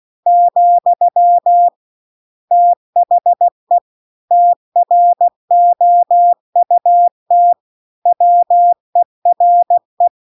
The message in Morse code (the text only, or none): , THE TROUT WERE